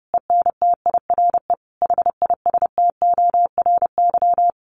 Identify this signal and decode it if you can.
Morse code, heard as a keyed tone, about 30 wpm, 700 Hz. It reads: ENTIRE HISTORY
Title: A Sound Of Thunder 30wpm